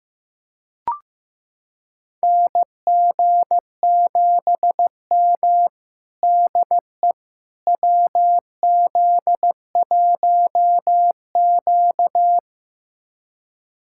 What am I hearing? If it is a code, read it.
Morse code, heard as a keyed tone, about 16 wpm, 700 Hz. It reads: NG7M DE WZ1Q
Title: USA Callsign DE Callsign - 4 Characters - 15wpm